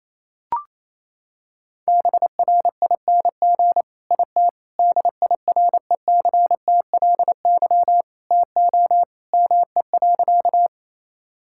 Morse code, heard as a keyed tone, about 28 wpm, 700 Hz. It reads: BRING IT DIRECTLY TO ME.
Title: Sentences from Top 1000 Words - Review Part 5 - 28wpm